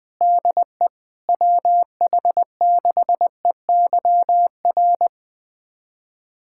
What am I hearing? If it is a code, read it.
Morse code, heard as a keyed tone, about 20 wpm, 700 Hz. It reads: DE WH6EYR
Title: USA Callsign DE Callsign 20wpm